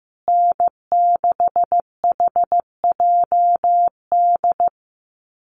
Morse code, heard as a keyed tone, about 15 wpm, 700 Hz. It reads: N6HJD
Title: USA Call Signs Repeated 3x 15wpm